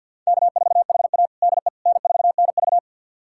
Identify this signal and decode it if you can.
Morse code, heard as a keyed tone, about 46 wpm, 700 Hz. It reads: K4HA DE N4NV
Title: USA Callsign DE Callsign - 4 Characters - 50wpm